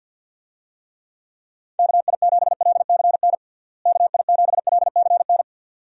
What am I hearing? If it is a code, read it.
Morse code, heard as a keyed tone, about 50 wpm, 700 Hz. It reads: KI7LCN KI7LCN
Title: USA Call Signs - 6 Characters - Repeated 2x - 50wpm